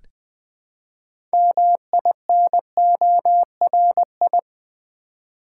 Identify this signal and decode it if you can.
Morse code, heard as a keyed tone, about 20 wpm, 700 Hz. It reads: MINORI
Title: Build A Word - 3 Letters At A Time - 8-Letter Words - 20wpm